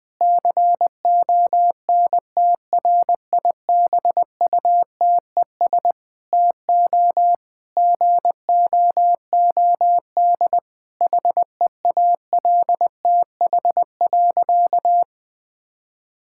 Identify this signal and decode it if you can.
Morse code, heard as a keyed tone, about 20 wpm, 700 Hz. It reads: CONTRIBUTES TO GOOD HEALTH.
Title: Difficult Sentences with Hints - 20wpm